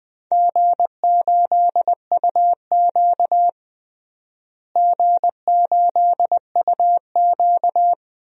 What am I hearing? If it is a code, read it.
Morse code, heard as a keyed tone, about 20 wpm, 700 Hz. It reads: G8UQ G8UQ
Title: DXCC-100 Call Signs - 4 Characters - Repeated 2x - 20wpm